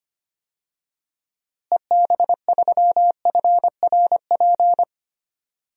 Morse code, heard as a keyed tone, about 25 wpm, 700 Hz. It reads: EB3FRP
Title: DXCC-100 Call Signs - 6 Characters - Repeated 2x - 25wpm